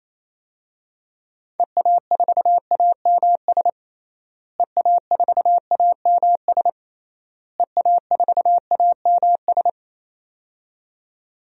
Morse code, heard as a keyed tone, about 28 wpm, 700 Hz. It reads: EA4AMS EA4AMS EA4AMS
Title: DXCC-100 Call Signs - 6 Characters - Repeated 3x - 28wpm